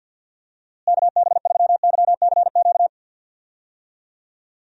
Morse code, heard as a keyed tone, about 50 wpm, 700 Hz. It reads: KB3YCQ
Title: USA Call Signs - 6 Characters - Repeated 3x - 50wpm